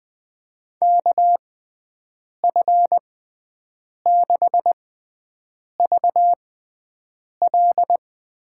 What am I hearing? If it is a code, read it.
Morse code, heard as a keyed tone, about 20 wpm, spaced as for 7 wpm, 700 Hz. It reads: KF6VL
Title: USA Call Signs - 6 Characters - 6x Character Spacing - 20wpm